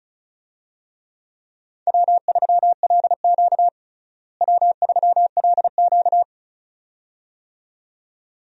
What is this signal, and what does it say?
Morse code, heard as a keyed tone, about 36 wpm, 700 Hz. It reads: W3LQ W3LQ
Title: USA Call Signs - 4 Characters - Repeated 2x - 35wpm